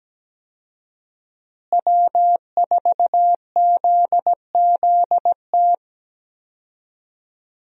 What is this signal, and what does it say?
Morse code, heard as a keyed tone, about 17 wpm, 700 Hz. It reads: W4ZZT
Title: USA Call Signs - 5 Characters - Repeated 3x - 17wpm